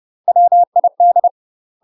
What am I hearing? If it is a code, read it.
Morse code, heard as a keyed tone, about 30 wpm, 700 Hz. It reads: WID
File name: UK MASTER QSOs TRAINING/30WPM